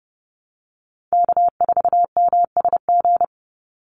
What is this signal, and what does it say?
Morse code, heard as a keyed tone, about 30 wpm, 700 Hz. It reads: K4MSG
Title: Call Signs Repeated 2x - 30wpm